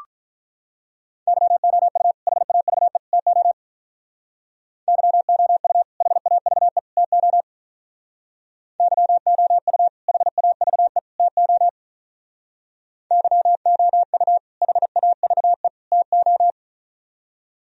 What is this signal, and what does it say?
Morse code, heard as a keyed tone, about 43 wpm, 700 Hz. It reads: YOU HAVE TO YOU HAVE TO YOU HAVE TO YOU HAVE TO
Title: Speed-Racing - Sets of 3 Words 35wpm